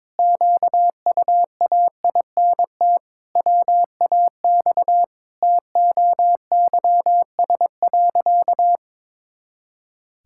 Morse code, heard as a keyed tone, about 22 wpm, 700 Hz. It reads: QUAINT WAX TOYS.
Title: Pangram Challenge v2 - 22wpm